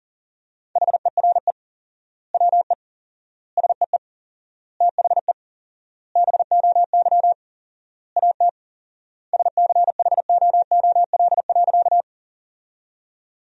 Morse code, heard as a keyed tone, about 40 wpm, 700 Hz. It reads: HERE WE SEE THE BOY AT SCHOOL.
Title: 4x Word-Spacing - Sentences from Top 300 Words - 40wpm